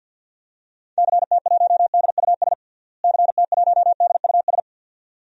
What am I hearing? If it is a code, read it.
Morse code, heard as a keyed tone, about 48 wpm, 700 Hz. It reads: CT1DUS CT1DUS
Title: DXCC-100 Call Signs - 6 Characters - Repeated 2x - 50wpm